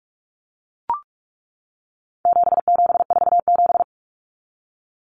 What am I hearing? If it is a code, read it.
Morse code, heard as a keyed tone, about 44 wpm, 700 Hz. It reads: 7747
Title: Common Cut Numbers - 4-Digit 45wpm